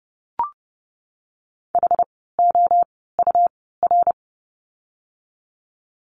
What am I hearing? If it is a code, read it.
Morse code, heard as a keyed tone, about 30 wpm, spaced as for 17 wpm, 700 Hz. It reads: HOUR
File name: course / 30:10wpm / 108 Lesson 18f - Letter m - Words 30wpm